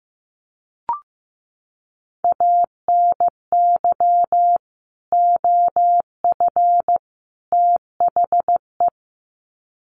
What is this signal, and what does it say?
Morse code, heard as a keyed tone, about 15 wpm, 700 Hz. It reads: ANY OF THE
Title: Sets of 3 Words 15wpm